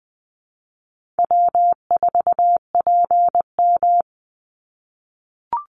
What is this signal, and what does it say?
Morse code, heard as a keyed tone, about 20 wpm, 700 Hz. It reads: W4PM
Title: Call Signs - 20wpm